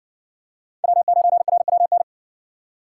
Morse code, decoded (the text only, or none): W9RWN